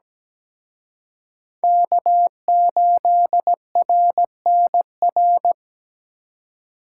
Morse code, heard as a keyed tone, about 20 wpm, 700 Hz. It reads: K8RNR
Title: Speed Racing - USA 1x3 Call Signs 17wpm